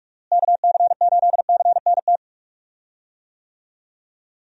Morse code, heard as a keyed tone, about 44 wpm, 700 Hz. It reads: KC8CNT
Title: USA Call Signs - 6 Characters - Repeated 2x - 45wpm